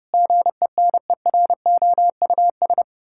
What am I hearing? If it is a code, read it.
Morse code, heard as a keyed tone, about 30 wpm, 700 Hz. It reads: GENEROUS
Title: Pangram Challenge v2 - 30wpm